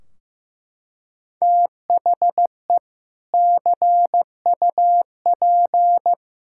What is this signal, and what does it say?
Morse code, heard as a keyed tone, about 15 wpm, 700 Hz. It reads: THE CUP
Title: Sentences from Top 1600 Words 15wpm